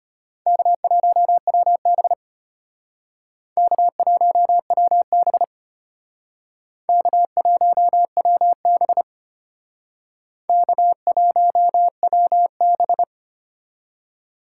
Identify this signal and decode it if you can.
Morse code, heard as a keyed tone, about 32 wpm, 700 Hz. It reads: K1WB K1WB K1WB K1WB